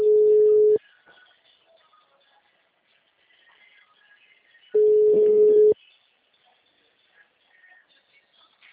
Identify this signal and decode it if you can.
Ringback tone (Europe, 425 Hz).